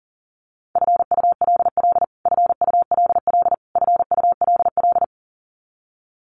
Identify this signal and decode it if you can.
Morse code, heard as a keyed tone, about 40 wpm, 700 Hz. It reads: FULL FULL FULL
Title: Top 500 Words Repeated 40wpm